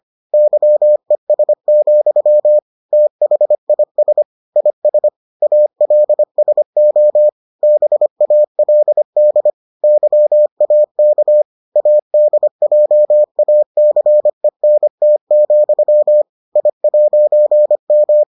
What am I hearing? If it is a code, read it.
Morse code, heard as a keyed tone, about 25 wpm, 600 Hz. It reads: YES, THIS IS ALSO BALD YAK ADJACENT, I'M